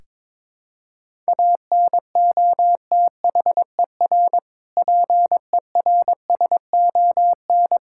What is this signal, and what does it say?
Morse code, heard as a keyed tone, about 21 wpm, 700 Hz. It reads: ANOTHER PERSON
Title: Sentences from Top 1000 Words - Review Part 5 - 22wpm